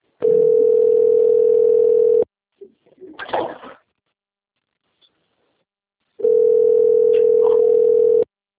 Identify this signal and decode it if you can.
Ringback tone (North America), 440 + 480 Hz.
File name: rg-1600-unknown- - - .12876